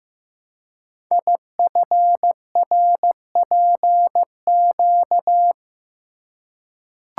Morse code, heard as a keyed tone, about 15 wpm, 700 Hz. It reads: IFRPQ
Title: Code Groups - 5 Letter:Numbers - 15wpm